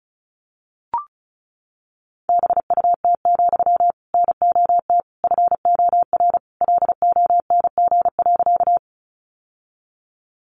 Morse code, heard as a keyed tone, about 34 wpm, 700 Hz. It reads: BUT, NOT FOR LONG.
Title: Sentences from Top 600 Words 35wpm